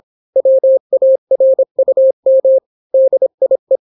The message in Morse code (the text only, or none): WARUM DIE